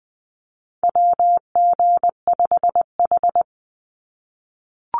Morse code, heard as a keyed tone, about 19 wpm, 700 Hz. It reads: WG5H
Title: Call Signs - 20wpm